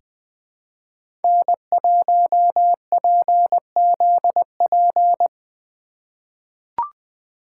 Morse code, heard as a keyed tone, about 20 wpm, 700 Hz. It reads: N1PZP
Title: USA Call Signs - 5 Characters - 4x Character Spacing - 20wpm